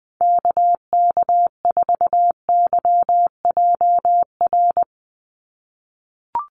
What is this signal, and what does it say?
Morse code, heard as a keyed tone, about 20 wpm, 700 Hz. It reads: KK4YJR
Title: USA Call Signs 20wpm